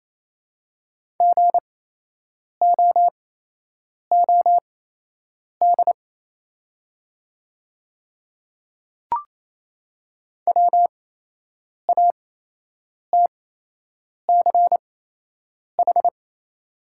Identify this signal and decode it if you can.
Morse code, heard as a keyed tone, about 28 wpm, spaced as for 8 wpm, 700 Hz. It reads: GOOD WATCH
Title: 8x Character Spacing - Top 500 Words - 28wpm